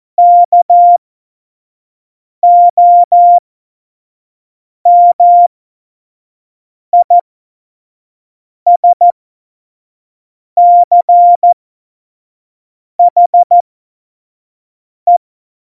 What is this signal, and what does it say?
Morse code, heard as a keyed tone, about 14 wpm, spaced as for 5 wpm, 700 Hz. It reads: KOMISCHE
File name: Morsekurs / Witze / Witz 05. September 17, Tempo 25